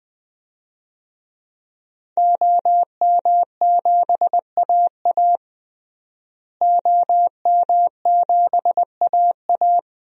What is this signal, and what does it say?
Morse code, heard as a keyed tone, about 20 wpm, 700 Hz. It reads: OM7AA OM7AA
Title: DXCC-100 Call Signs - 5 Characters - Repeated 3x - 20wpm